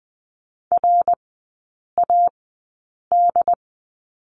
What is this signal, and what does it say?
Morse code, heard as a keyed tone, about 20 wpm, spaced as for 8 wpm, 700 Hz. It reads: RAD
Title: 138 Lesson 23e - Letter f - Three Characters 20wpm